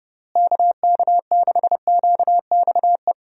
Morse code, heard as a keyed tone, about 30 wpm, 700 Hz. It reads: KK6QXE
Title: USA Call Signs - 6 Characters - Repeated 2x - 30wpm